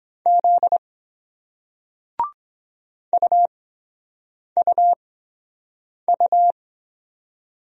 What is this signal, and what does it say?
Morse code, heard as a keyed tone, about 23 wpm, spaced as for 7 wpm, 700 Hz. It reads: Z UUU